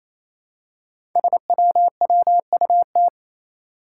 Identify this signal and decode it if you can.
Morse code, heard as a keyed tone, about 29 wpm, 700 Hz. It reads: SWWUT
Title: Five Challenging Letter-Number Combinations 28wpm